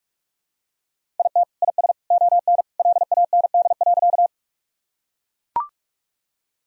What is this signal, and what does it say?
Morse code, heard as a keyed tone, about 41 wpm, 700 Hz. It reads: IT IS ON LAND.
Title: Sentences from Top 1000 Words - Review Part 1 - 45wpm